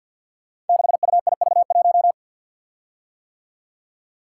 Morse code, heard as a keyed tone, about 52 wpm, 700 Hz. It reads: 6UIV1
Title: Code Groups - 5 Letter:Numbers - 50wpm